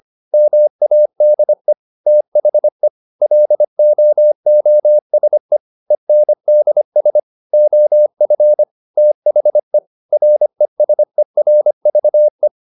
Morse code, heard as a keyed tone, about 25 wpm, 600 Hz. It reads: MADE THE LOOSE ENDS OF THE RESERVE
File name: the old man and the sea part 1 25wpm